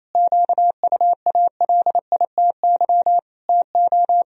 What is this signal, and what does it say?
Morse code, heard as a keyed tone, about 28 wpm, 700 Hz. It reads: QUALITY TO
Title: Sentences from Top 1000 Words - Review Part 4 - 28wpm